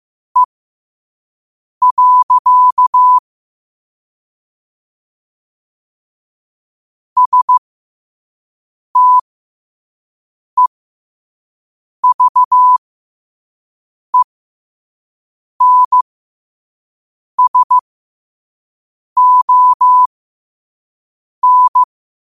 Morse code, heard as a keyed tone, about 15 wpm, spaced as for 5 wpm, 1000 Hz. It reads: E. STEVENSON
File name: qotdfiles / 5wpm